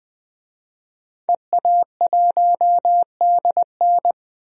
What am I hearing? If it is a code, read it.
Morse code, heard as a keyed tone, about 20 wpm, 700 Hz. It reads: EA1DN
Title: DXCC-100 Call Signs - 5 Characters - 20wpm